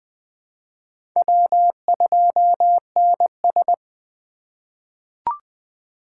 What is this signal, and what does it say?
Morse code, heard as a keyed tone, about 20 wpm, 700 Hz. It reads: W2NS
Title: USA Call Signs - 4 Characters - 8x Character Spacing - 20wpm